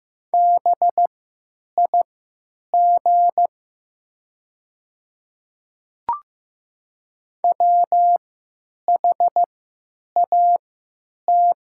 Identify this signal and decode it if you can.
Morse code, heard as a keyed tone, about 15 wpm, spaced as for 9 wpm, 700 Hz. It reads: BIG WHAT
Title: 3x Character Spacing - Top 300 Words - 15wpm